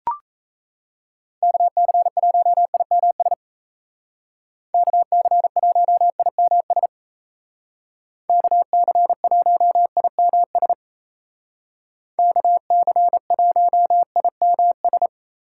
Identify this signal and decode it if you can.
Morse code, heard as a keyed tone, about 37 wpm, 700 Hz. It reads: KC1IMS KC1IMS KC1IMS KC1IMS